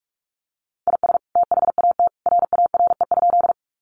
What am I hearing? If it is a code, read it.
Morse code, heard as a keyed tone, about 44 wpm, 700 Hz. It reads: IS THAT RARE?
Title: Sentences from Top 1100 Words 45wpm